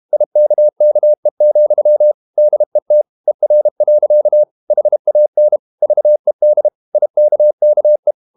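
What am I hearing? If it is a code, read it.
Morse code, heard as a keyed tone, about 32 wpm, 600 Hz. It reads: IKKE, DET ER. HAN VED IKKE
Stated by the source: High-speed QRQ CW at 32 WPM.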